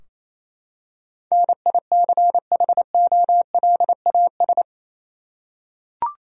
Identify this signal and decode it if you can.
Morse code, heard as a keyed tone, about 27 wpm, 700 Hz. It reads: NICHOLAS
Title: USA Common Names - 28wpm